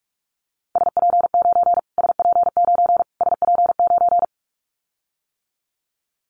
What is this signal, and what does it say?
Morse code, heard as a keyed tone, about 44 wpm, 700 Hz. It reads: SP9 SP9 SP9
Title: Callsign Prefixes for Top 25 DXCC - Repeated 3x - 45wpm